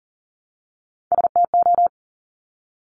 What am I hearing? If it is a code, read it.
Morse code, heard as a keyed tone, about 40 wpm, 700 Hz. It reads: STO
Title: Top Three Letter Combinations in Words 40wpm